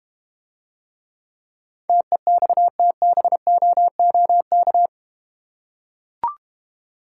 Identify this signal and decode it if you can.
Morse code, heard as a keyed tone, about 30 wpm, 700 Hz. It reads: TEXTBOOK
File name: Easy Words - Speed-Racing with Hints - 25wpm